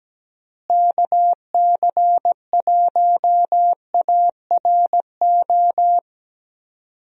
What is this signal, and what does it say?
Morse code, heard as a keyed tone, about 17 wpm, 700 Hz. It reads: KC1ARO
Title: USA Call Signs - 6 Characters - Repeated 3x - 17wpm